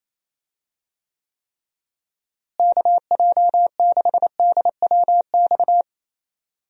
Morse code, heard as a keyed tone, about 28 wpm, 700 Hz. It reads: KJ6DWX
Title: USA Call Signs - 6 Characters - Repeated 3x - 28wpm